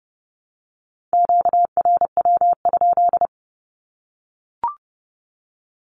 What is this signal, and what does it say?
Morse code, heard as a keyed tone, about 30 wpm, 700 Hz. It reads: QRW?
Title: Q-Codes Repeated 3x 30wpm